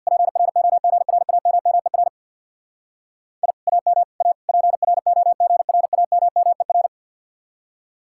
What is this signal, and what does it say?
Morse code, heard as a keyed tone, about 64 wpm, 700 Hz. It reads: PROGRAMMER I AM A PROGRAMMER